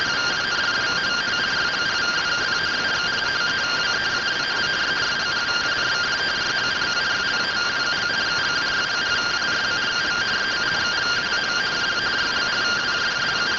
Two-tone teleprinter signal, about 170 Hz shift.